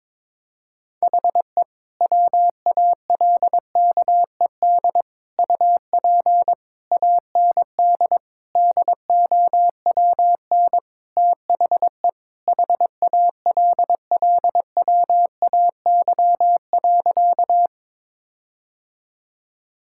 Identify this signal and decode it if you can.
Morse code, heard as a keyed tone, about 21 wpm, 700 Hz. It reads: HE WALKED UP AND DOWN THE HALLWAY.
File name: practice / 22wpm / Sentences with 4-Word Phrases - 22wpm